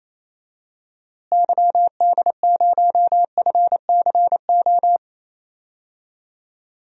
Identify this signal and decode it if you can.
Morse code, heard as a keyed tone, about 28 wpm, 700 Hz. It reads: YD0FCO
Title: DXCC-100 Call Signs - 6 Characters - Repeated 2x - 28wpm